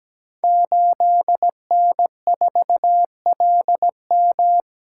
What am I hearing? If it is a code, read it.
Morse code, heard as a keyed tone, about 17 wpm, 700 Hz. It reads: 8N4LM
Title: Code Groups - 5 Letter:Numbers - 17wpm